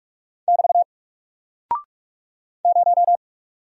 Morse code, heard as a keyed tone, about 44 wpm, 700 Hz. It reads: <BT> 0
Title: Single Character (Letter:Number:Punctuation:Prosign) 45wpm